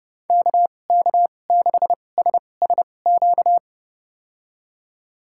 Morse code, heard as a keyed tone, about 30 wpm, spaced as for 22 wpm, 700 Hz. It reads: KK6SSQ
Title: USA Call Signs - 6 Characters - 2x Character Spacing - 30wpm